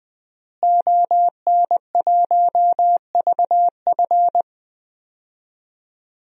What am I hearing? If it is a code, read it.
Morse code, heard as a keyed tone, about 20 wpm, 700 Hz. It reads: ON1VF
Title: DXCC-100 Call Signs - 5 Characters - Repeated 3x - 20wpm